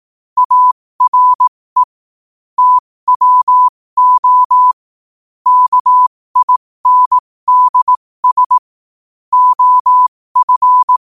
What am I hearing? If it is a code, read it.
Morse code, heard as a keyed tone, about 18 wpm, spaced as for 15 wpm, 1000 Hz. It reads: ARE TWO KINDS OF